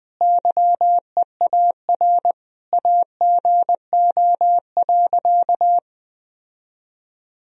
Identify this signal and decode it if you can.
Morse code, heard as a keyed tone, about 21 wpm, 700 Hz. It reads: YEAR AGO.
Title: Sentences from Top 1000 Words - Review Part 3 - 20wpm